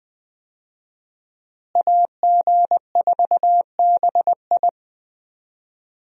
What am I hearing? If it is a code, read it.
Morse code, heard as a keyed tone, about 21 wpm, 700 Hz. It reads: AG4BI
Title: USA Call Signs - 5 Characters - Repeated 3x - 20wpm